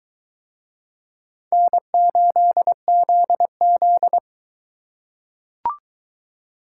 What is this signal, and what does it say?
Morse code, heard as a keyed tone, about 22 wpm, 700 Hz. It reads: N8ZZ